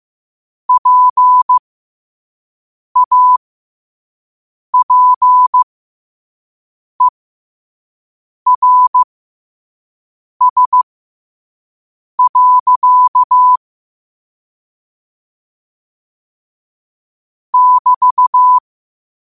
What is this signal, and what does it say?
Morse code, heard as a keyed tone, about 15 wpm, spaced as for 5 wpm, 1000 Hz. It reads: PAPERS. <BT>
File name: qotdfiles / 5wpm